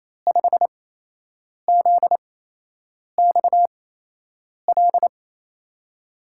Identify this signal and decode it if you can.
Morse code, heard as a keyed tone, about 28 wpm, spaced as for 8 wpm, 700 Hz. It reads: 5ZXL
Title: USA Call Signs - 6 Characters - 8x Character Spacing - 28wpm